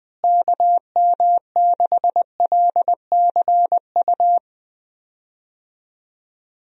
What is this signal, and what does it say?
Morse code, heard as a keyed tone, about 20 wpm, 700 Hz. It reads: KM6LCU
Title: USA Call Signs - 6 Characters - Repeated 3x - 20wpm